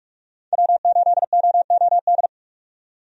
Morse code, heard as a keyed tone, about 46 wpm, 700 Hz. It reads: W8OOD
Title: USA Call Signs - 5 Characters - 45wpm